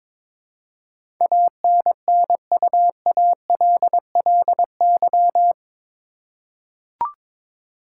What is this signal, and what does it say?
Morse code, heard as a keyed tone, about 22 wpm, 700 Hz. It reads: ANNUALLY
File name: practice / 22wpm / Eight Letter Words - 22wpm